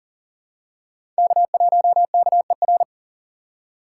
Morse code, heard as a keyed tone, about 40 wpm, 700 Hz. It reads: K1KER
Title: USA Call Signs - 5 Characters - 2x Character Spacing - 40wpm